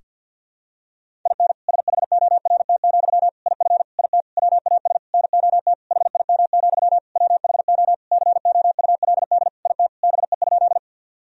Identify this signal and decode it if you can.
Morse code, heard as a keyed tone, about 47 wpm, 700 Hz. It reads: IN SHORT, IF IT WAS NOT HIM, WHO COULD IT BE?
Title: Sentences from Top 1000 Words - Review Part 2 - 50wpm